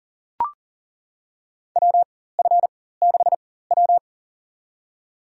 Morse code, heard as a keyed tone, about 40 wpm, spaced as for 19 wpm, 700 Hz. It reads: WF6W
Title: USA Call Signs - 4x Character Spacing - 40wpm